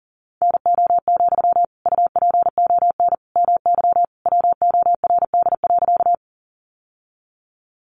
Morse code, heard as a keyed tone, about 40 wpm, 700 Hz. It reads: NO, UPON MY WORD.
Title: Sentences from Top 1100 Words 40wpm